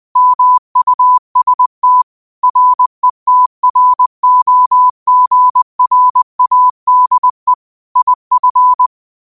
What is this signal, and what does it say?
Morse code, heard as a keyed tone, about 20 wpm, 1000 Hz. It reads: MUST RETROGRADE IF
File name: qotdfiles / 20wpm